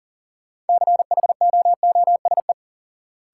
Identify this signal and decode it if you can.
Morse code, heard as a keyed tone, about 40 wpm, 700 Hz. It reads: CHOOSE
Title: Sentences from Top 700 Words - New Word Drill - 40wpm